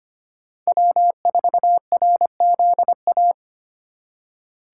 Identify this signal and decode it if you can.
Morse code, heard as a keyed tone, about 25 wpm, 700 Hz. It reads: W4RZA
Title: USA Call Signs - 5 Characters - 4x Character Spacing - 25wpm